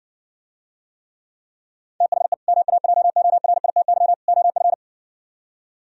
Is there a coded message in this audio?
Yes: Morse code from 2.0 to 4.7 s, heard as a keyed tone, about 59 wpm, 700 Hz. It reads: THE MAJORITY OF